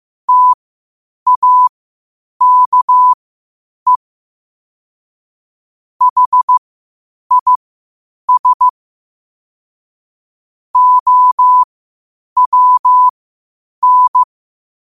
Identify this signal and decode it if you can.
Morse code, heard as a keyed tone, about 15 wpm, spaced as for 8 wpm, 1000 Hz. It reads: TAKE HIS OWN
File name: qotdfiles / 8wpm